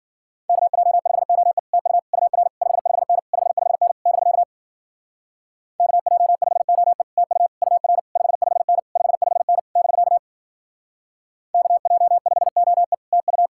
Morse code, heard as a keyed tone, about 52 wpm, 700 Hz. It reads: KJ5OE TU UR 55N 55N <BK> KJ5OE TU UR 55N 55N <BK> KJ5OE TU